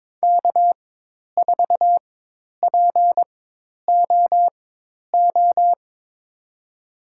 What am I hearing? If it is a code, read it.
Morse code, heard as a keyed tone, about 22 wpm, spaced as for 10 wpm, 700 Hz. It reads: K4POO
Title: USA Call Signs - 5 Characters - 4x Character Spacing - 22wpm